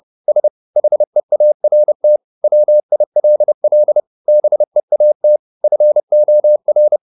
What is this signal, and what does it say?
Morse code, heard as a keyed tone, about 30 wpm, 600 Hz. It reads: S HEART WILL BEAT FOR
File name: the old man and the sea / the old man and the sea 1 30wpm